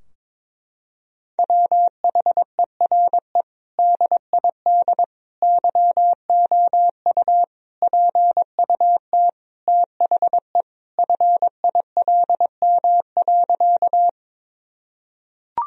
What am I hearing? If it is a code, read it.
Morse code, heard as a keyed tone, about 22 wpm, 700 Hz. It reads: WHERE DID YOU PUT THE FILM.
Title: Sentences from Top 1000 Words - Review Part 3 - 22wpm